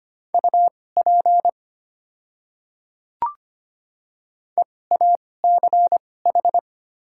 Morse code, heard as a keyed tone, about 25 wpm, spaced as for 18 wpm, 690 Hz. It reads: UP EACH
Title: 2x Character Spacing - Top 100 Words - 25wpm